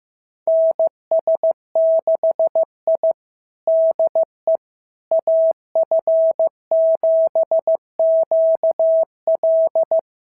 Morse code, heard as a keyed tone, about 15 wpm, 650 Hz. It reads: NS6I DE AF7QL
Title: SOTA QSO Practice 15wpm